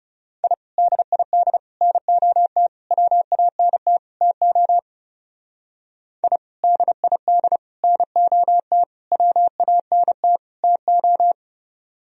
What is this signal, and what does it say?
Morse code, heard as a keyed tone, about 37 wpm, 700 Hz. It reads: I DID NOT WANT TO I DID NOT WANT TO
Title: Speed-Racing - Sets of 5 Words 30wpm